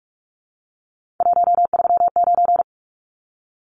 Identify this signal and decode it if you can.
Morse code, heard as a keyed tone, about 44 wpm, 700 Hz. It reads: 139
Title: Numbers - 3-Digit 45wpm